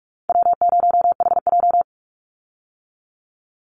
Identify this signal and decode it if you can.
Morse code, heard as a keyed tone, about 44 wpm, 700 Hz. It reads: W0HJ
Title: USA Call Signs 45wpm